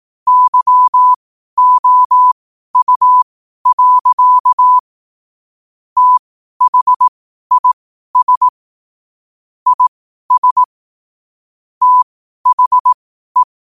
Morse code, heard as a keyed tone, about 18 wpm, spaced as for 12 wpm, 1000 Hz. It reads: YOU. THIS IS THE
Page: 12 wpm morse code quotes